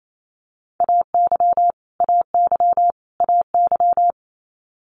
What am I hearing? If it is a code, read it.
Morse code, heard as a keyed tone, about 28 wpm, 700 Hz. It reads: AY AY AY
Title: Multi-letter Phonemes Repeated 3x 28wpm